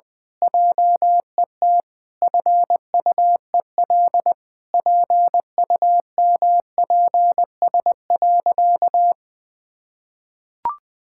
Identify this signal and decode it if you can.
Morse code, heard as a keyed tone, about 21 wpm, 700 Hz. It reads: JET FUEL PUMPS.
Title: Pangram Challenge v2 - 20wpm